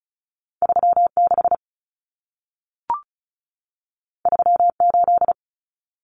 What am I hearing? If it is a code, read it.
Morse code, heard as a keyed tone, about 36 wpm, 700 Hz. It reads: 36 38
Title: Common Cut Numbers - 2-Digit 35wpm